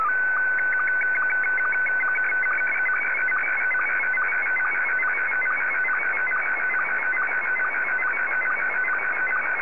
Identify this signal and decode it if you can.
SSTV (slow-scan television), Scottie 1.